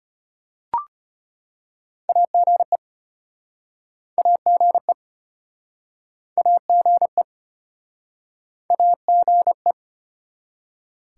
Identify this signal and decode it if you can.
Morse code, heard as a keyed tone, about 36 wpm, 700 Hz. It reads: AGE AGE AGE AGE